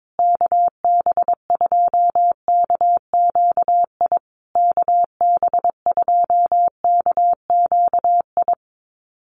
Morse code, heard as a keyed tone, about 22 wpm, 700 Hz. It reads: KB2KQI KB2KQI
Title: USA Call Signs Repeated 3x 22wpm